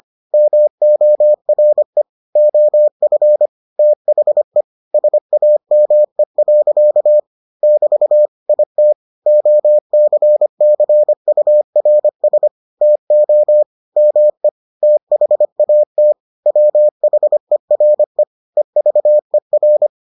Morse code, heard as a keyed tone, about 25 wpm, 600 Hz. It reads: MORE OF THE SAME. <BT> IT OCCURS TO ME THAT WHERE EVER